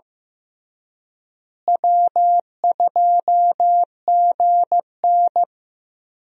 Morse code, heard as a keyed tone, about 15 wpm, 700 Hz. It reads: W2GN
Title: USA Call Signs - 4 Characters - Repeated 2x - 15wpm